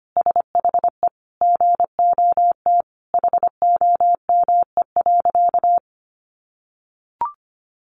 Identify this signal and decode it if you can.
Morse code, heard as a keyed tone, about 25 wpm, 700 Hz. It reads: SHE GOT HOME.
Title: Sentences from Top 1400 Words 25wpm